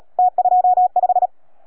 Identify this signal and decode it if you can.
Morse code, heard as a keyed tone, about 33 wpm, 710 Hz. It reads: T25